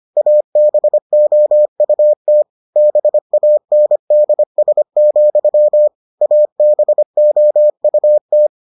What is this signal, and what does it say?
Morse code, heard as a keyed tone, about 25 wpm, 600 Hz. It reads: ABOUT BANDS, ABOUT